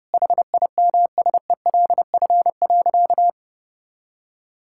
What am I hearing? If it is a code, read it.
Morse code, heard as a keyed tone, about 30 wpm, 700 Hz. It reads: HIMSELF.
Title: Sentences from Top 1600 Words 30wpm